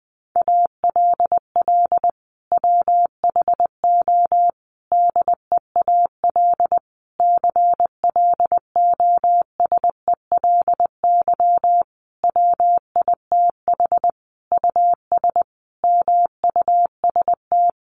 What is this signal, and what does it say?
Morse code, heard as a keyed tone, about 19 wpm, 700 Hz. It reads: ALL WHO DEAL CLOSELY WITH US MUST
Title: Sentences from Top 1400 Words 20wpm